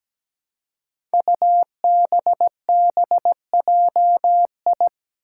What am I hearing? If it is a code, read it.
Morse code, heard as a keyed tone, about 17 wpm, 700 Hz. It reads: UBBJI
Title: Five Challenging Letter-Number Combinations 17wpm